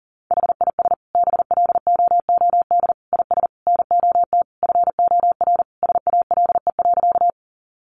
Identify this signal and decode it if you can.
Morse code, heard as a keyed tone, about 40 wpm, 700 Hz. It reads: HIS BLOOD IS NOT FOR SALE.
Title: Sentences from Top 1400 Words 40wpm